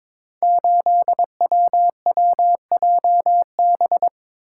Morse code, heard as a keyed tone, about 22 wpm, 700 Hz. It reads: 8WWJB
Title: Code Groups - 5 Letter:Numbers - 22wpm